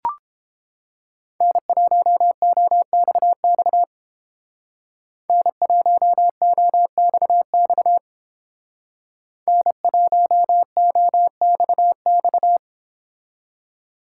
Morse code, heard as a keyed tone, about 29 wpm, 700 Hz. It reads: N1OXX N1OXX N1OXX